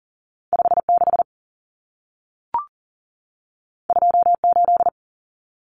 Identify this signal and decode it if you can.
Morse code, heard as a keyed tone, about 40 wpm, 700 Hz. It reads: 56 28
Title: Numbers - 2-Digit 40wpm